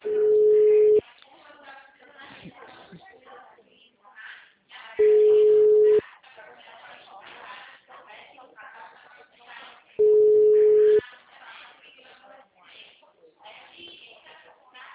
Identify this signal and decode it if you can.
Ringback tone (Europe, 425 Hz), with speech.
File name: out- -2013- - - .36951